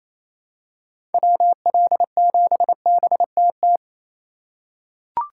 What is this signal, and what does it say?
Morse code, heard as a keyed tone, about 28 wpm, 700 Hz. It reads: WL7BTT
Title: USA Call Signs - 6 Characters - 3x Character Spacing - 28wpm